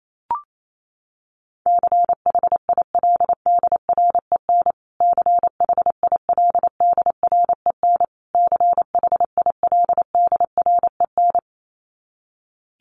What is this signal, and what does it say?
Morse code, heard as a keyed tone, about 28 wpm, 700 Hz. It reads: CHILDREN CHILDREN CHILDREN
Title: Top 300 Words Repeated 28wpm